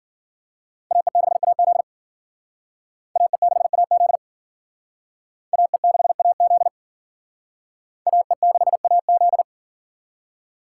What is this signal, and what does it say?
Morse code, heard as a keyed tone, about 50 wpm, 700 Hz. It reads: AE6AZ AE6AZ AE6AZ AE6AZ